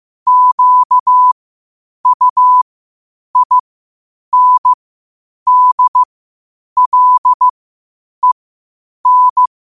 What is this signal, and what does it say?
Morse code, heard as a keyed tone, about 15 wpm, spaced as for 8 wpm, 1000 Hz. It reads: QUINDLEN
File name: qotdfiles / 8wpm